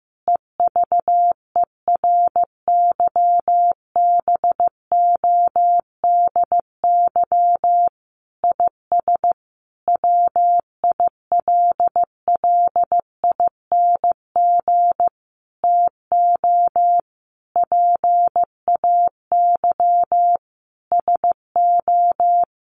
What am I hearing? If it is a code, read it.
Morse code, heard as a keyed tone, about 15 wpm, 700 Hz. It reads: EVERYBODY IS WILLING TO PAY SO
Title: Sentences from Top 1400 Words 15wpm